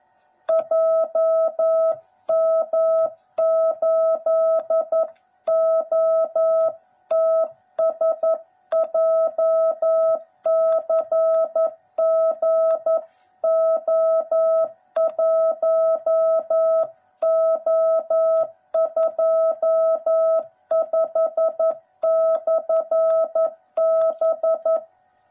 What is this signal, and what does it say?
Morse code, heard as a keyed tone, about 11 wpm, 660 Hz. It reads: JM8OTS JCGO1O25/B
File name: jh8cbh / 2021cw / cw09b